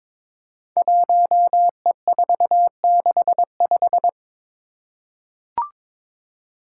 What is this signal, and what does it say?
Morse code, heard as a keyed tone, about 22 wpm, 700 Hz. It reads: 1E465
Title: Code Groups - 5 Letter:Numbers - 22wpm